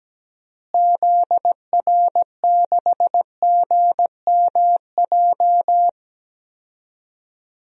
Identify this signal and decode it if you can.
Morse code, heard as a keyed tone, about 17 wpm, 700 Hz. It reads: ZR6GMJ
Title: DXCC-100 Call Signs - 6 Characters - Repeated 3x - 17wpm